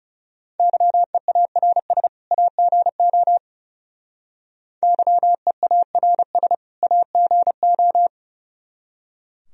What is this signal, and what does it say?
Morse code, heard as a keyed tone, about 35 wpm, 700 Hz. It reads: YEARS AGO YEARS AGO
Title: Speed-Racing - Sets of 2 Words 30wpm